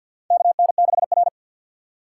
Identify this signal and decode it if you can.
Morse code, heard as a keyed tone, about 51 wpm, 700 Hz. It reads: KN6R
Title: USA Call Signs - 4 Characters - 2x Character Spacing - 50wpm